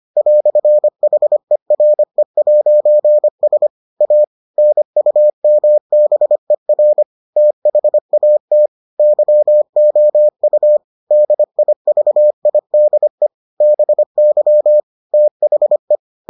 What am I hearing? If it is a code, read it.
Morse code, heard as a keyed tone, about 25 wpm, 600 Hz. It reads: "HERE'S A NUMBER THAT YOU DIVIDE BY THE